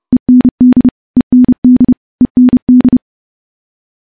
Morse code, heard as a keyed tone, about 30 wpm, 250 Hz. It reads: END END END
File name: morse / morse-training-30WPM